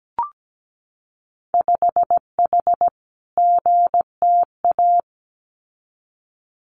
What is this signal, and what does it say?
Morse code, heard as a keyed tone, about 17 wpm, 700 Hz. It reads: 5H GTA
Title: Winter Field Day Exchanges Encore 17wpm